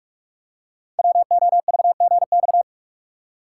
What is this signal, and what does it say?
Morse code, heard as a keyed tone, about 44 wpm, 700 Hz. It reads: WOVGX
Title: Code Groups - 5 Letters - 45wpm